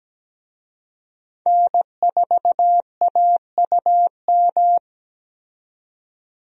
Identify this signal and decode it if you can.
Morse code, heard as a keyed tone, about 17 wpm, 700 Hz. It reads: N4AUM
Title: USA Call Signs - 5 Characters - 17wpm